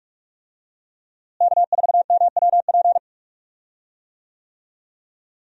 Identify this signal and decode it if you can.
Morse code, heard as a keyed tone, about 46 wpm, 700 Hz. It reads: K4MWP
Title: USA Call Signs - 5 Characters - Repeated 3x - 45wpm